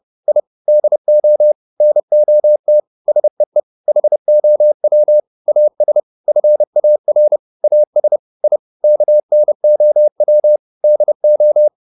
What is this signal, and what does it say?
Morse code, heard as a keyed tone, about 30 wpm, 600 Hz. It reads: I DO NOT SEE HOW AS FAR AS I KNOW DO